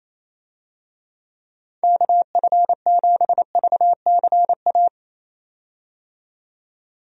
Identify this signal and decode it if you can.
Morse code, heard as a keyed tone, about 27 wpm, 700 Hz. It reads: KF7VCA
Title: USA Call Signs - 6 Characters - 28wpm